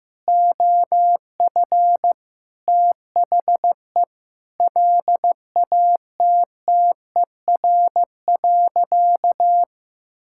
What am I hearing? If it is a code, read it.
Morse code, heard as a keyed tone, about 15 wpm, 700 Hz. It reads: OF THE LATTER.
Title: Sentences from Top 1600 Words 15wpm